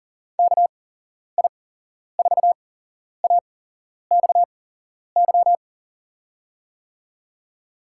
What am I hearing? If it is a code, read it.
Morse code, heard as a keyed tone, about 38 wpm, spaced as for 11 wpm, 690 Hz. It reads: KI4AXY
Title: USA Call Signs - 8x Character Spacing - 40wpm